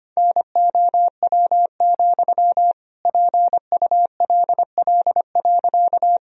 Morse code, heard as a keyed tone, about 25 wpm, 700 Hz. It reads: NOW, PULL.
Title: Sentences from Top 1000 Words - Review Part 2 - 25wpm